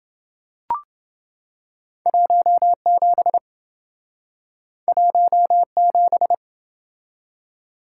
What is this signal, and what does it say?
Morse code, heard as a keyed tone, about 27 wpm, 700 Hz. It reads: 17 17